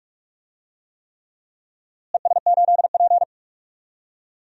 Morse code, heard as a keyed tone, about 46 wpm, 700 Hz. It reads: ES8P
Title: DXCC-100 Call Signs - 4 Characters - Repeated 3x - 45wpm